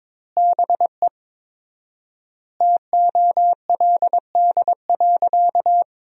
Morse code, heard as a keyed tone, about 22 wpm, 700 Hz. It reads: BE TOLD.
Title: 4x Word-Spacing - Sentences from Top 400 Words - 22wpm